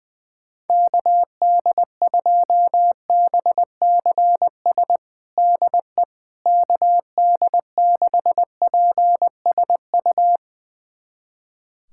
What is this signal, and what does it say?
Morse code, heard as a keyed tone, about 20 wpm, 700 Hz. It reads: KD2BCS DE KD6PSU
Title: USA Callsign DE Callsign 20wpm